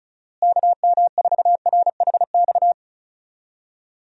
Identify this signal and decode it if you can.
Morse code, heard as a keyed tone, about 34 wpm, 700 Hz. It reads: KM4RHX
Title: USA Call Signs - 6 Characters - Repeated 3x - 35wpm